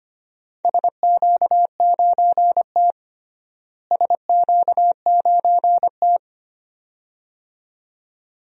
Morse code, heard as a keyed tone, about 25 wpm, 700 Hz. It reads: SQ9T SQ9T
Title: DXCC-100 Call Signs - 4 Characters - Repeated 2x - 25wpm